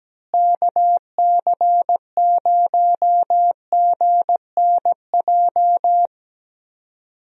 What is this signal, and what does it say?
Morse code, heard as a keyed tone, about 17 wpm, 700 Hz. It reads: KC0GNJ
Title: USA Call Signs - 6 Characters - Repeated 3x - 17wpm